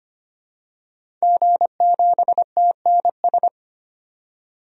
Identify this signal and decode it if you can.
Morse code, heard as a keyed tone, about 25 wpm, 700 Hz. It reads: G7TNS
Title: DXCC-100 Call Signs - 5 Characters - 25wpm